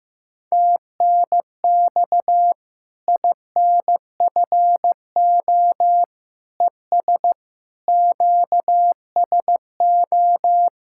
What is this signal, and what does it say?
Morse code, heard as a keyed tone, about 15 wpm, 700 Hz. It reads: TNX INFO ES QSO
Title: Practice QSO - Exchange 4 - Ending - 15wpm